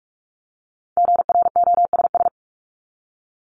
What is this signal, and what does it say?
Morse code, heard as a keyed tone, about 44 wpm, 700 Hz. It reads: GROSS
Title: Top 1751-2000 Words 45wpm